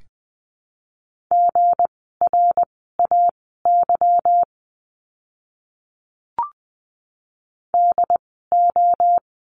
Morse code, heard as a keyed tone, about 20 wpm, spaced as for 15 wpm, 700 Hz. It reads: GRAY DO
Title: 160 Lesson 26f - Letter g - Words 20wpm